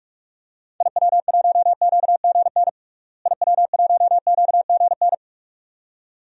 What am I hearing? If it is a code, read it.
Morse code, heard as a keyed tone, about 46 wpm, 700 Hz. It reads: IW1QGN IW1QGN
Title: DXCC-100 Call Signs - 6 Characters - Repeated 2x - 45wpm